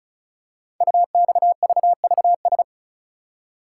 Morse code, heard as a keyed tone, about 36 wpm, 690 Hz. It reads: UXVVS
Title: Code Groups - 5 Letters - 35wpm